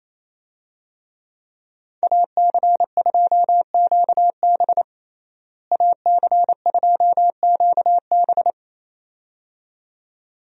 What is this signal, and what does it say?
Morse code, heard as a keyed tone, about 28 wpm, 700 Hz. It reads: AC2QB AC2QB
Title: USA Call Signs - 5 Characters - Repeated 2x - 28wpm